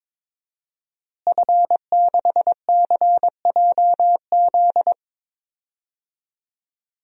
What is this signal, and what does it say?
Morse code, heard as a keyed tone, about 22 wpm, 700 Hz. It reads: F6CJZ
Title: DXCC-100 Call Signs - 5 Characters - Repeated 3x - 22wpm